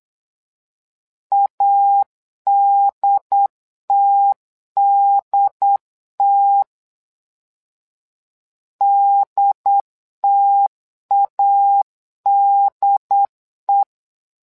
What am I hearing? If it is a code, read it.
Morse code, heard as a keyed tone, about 8 wpm, 800 Hz. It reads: ADTDT DTADE
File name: LEZIONI / 2 UHDN / 2 UHD 8wpm 2258